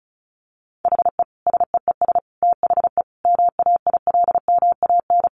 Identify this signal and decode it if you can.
Morse code, heard as a keyed tone, about 34 wpm, 700 Hz. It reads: HE SEES THE MAILMAN
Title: Additional Easy Phrases 35wpm